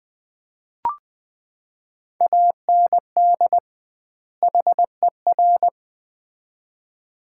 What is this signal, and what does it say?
Morse code, heard as a keyed tone, about 20 wpm, 700 Hz. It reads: AND HER
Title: 2x Word-Spacing - Sets of 2 Words - 20wpm